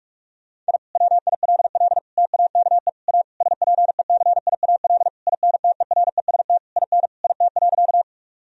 Morse code, heard as a keyed tone, about 42 wpm, 700 Hz. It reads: I WILL TAKE A SPECIAL INTEREST IN IT.
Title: Sentences from Top 1000 Words - Review Part 3 - 45wpm